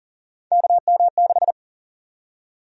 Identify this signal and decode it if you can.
Morse code, heard as a keyed tone, about 40 wpm, 700 Hz. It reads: KM6
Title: USA Call Sign Prefixes - 40wpm